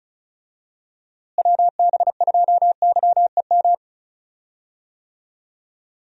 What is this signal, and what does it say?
Morse code, heard as a keyed tone, about 36 wpm, 700 Hz. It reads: WB2YEM
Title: USA Call Signs - 6 Characters - Repeated 3x - 35wpm